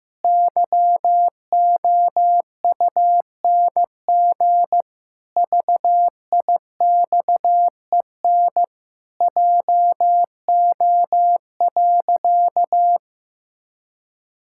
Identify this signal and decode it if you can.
Morse code, heard as a keyed tone, about 16 wpm, 700 Hz. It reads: YOUNG VIXEN JO.
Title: Pangram Challenge v2 - 15wpm